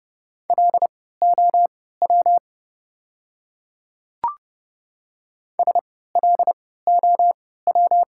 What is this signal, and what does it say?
Morse code, heard as a keyed tone, about 30 wpm, spaced as for 17 wpm, 700 Hz. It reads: LOW SLOW
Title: 3x Character Spacing - Top 500 Words - 30wpm